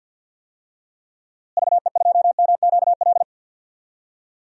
Morse code, heard as a keyed tone, about 51 wpm, 700 Hz. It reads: VE2MQL
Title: DXCC-100 Call Signs - 6 Characters - Repeated 3x - 50wpm